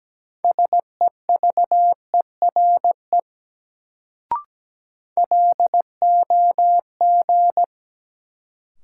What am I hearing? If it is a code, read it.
Morse code, heard as a keyed tone, about 17 wpm, 700 Hz. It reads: SEVERE LOG
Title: Sentences from Top 1400 Words - New Word Drill Alt - 17wpm